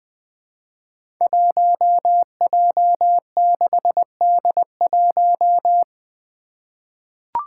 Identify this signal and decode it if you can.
Morse code, heard as a keyed tone, about 20 wpm, 700 Hz. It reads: 1J6D1
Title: Five Challenging Letter-Number Combinations 20wpm